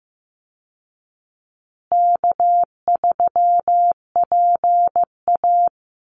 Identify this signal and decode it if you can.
Morse code, heard as a keyed tone, about 15 wpm, 700 Hz. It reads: K3PA
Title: Easy Call Signs Repeated 3x - 15wpm